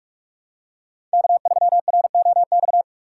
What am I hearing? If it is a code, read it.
Morse code, heard as a keyed tone, about 45 wpm, 700 Hz. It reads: K3ROX
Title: USA Call Signs - 5 Characters - 2x Character Spacing - 45wpm